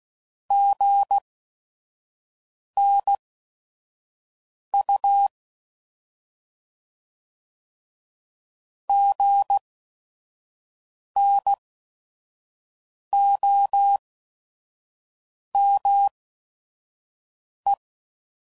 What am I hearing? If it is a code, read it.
Morse code, heard as a keyed tone, about 16 wpm, spaced as for 5 wpm, 800 Hz. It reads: GNU GNOME
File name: Prefixes / By First Letter / Gs